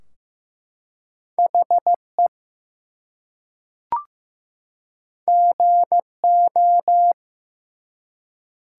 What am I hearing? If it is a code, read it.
Morse code, heard as a keyed tone, about 15 wpm, 700 Hz. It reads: HE GO
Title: Two Letter Words - 15wpm